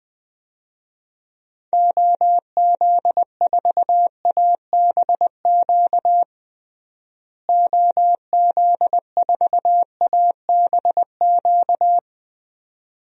Morse code, heard as a keyed tone, about 20 wpm, 700 Hz. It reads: OZ4ABQ OZ4ABQ
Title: DXCC-100 Call Signs - 6 Characters - Repeated 2x - 20wpm